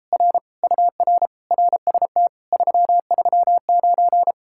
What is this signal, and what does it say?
Morse code, heard as a keyed tone, about 37 wpm, 700 Hz. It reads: R UR RST 339